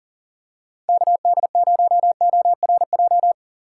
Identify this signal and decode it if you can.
Morse code, heard as a keyed tone, about 39 wpm, 700 Hz. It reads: KD0ORJ
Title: USA Call Signs - 6 Characters - 4x Character Spacing - 40wpm